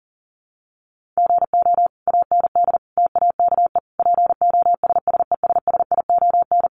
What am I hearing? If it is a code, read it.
Morse code, heard as a keyed tone, about 40 wpm, 700 Hz. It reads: GO AND TAKE POSSESSION
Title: Sentences from Top 1400 Words 40wpm